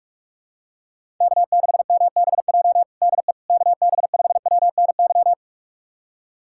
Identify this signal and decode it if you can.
Morse code, heard as a keyed tone, about 46 wpm, 700 Hz. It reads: K6MBJ DE KB5WNY
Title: USA Callsign DE Callsign 45wpm